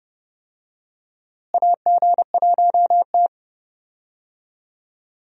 Morse code, heard as a keyed tone, about 30 wpm, 700 Hz. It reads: AG1T
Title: USA Call Signs - 4 Characters - Repeated 3x - 30wpm